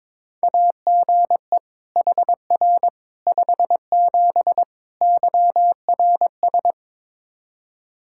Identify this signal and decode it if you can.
Morse code, heard as a keyed tone, about 22 wpm, 700 Hz. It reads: AGE HR 57 YRS
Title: Practice QSO - Exchange 3 - More Info - 22wpm